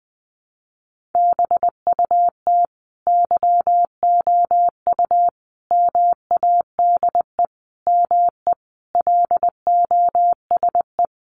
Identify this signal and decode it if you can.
Morse code, heard as a keyed tone, about 20 wpm, 700 Hz. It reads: BUT YOU MADE ME LOSE
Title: Sentences from Top 1300 Words 20wpm